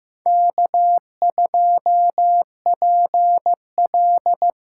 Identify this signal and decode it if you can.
Morse code, heard as a keyed tone, about 15 wpm, 700 Hz. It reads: K2PL
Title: USA Call Signs - 5 Characters - 6x Character Spacing - 15wpm